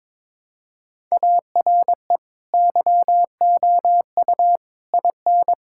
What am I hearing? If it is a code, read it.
Morse code, heard as a keyed tone, about 22 wpm, 700 Hz. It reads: ARE YOU IN
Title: Sentences from Top 1000 Words - Review Part 6 - 22wpm